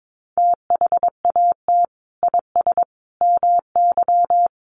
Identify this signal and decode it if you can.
Morse code, heard as a keyed tone, about 22 wpm, 700 Hz. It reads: THAT IS MY
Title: Sentences from Top 1200 Words 22wpm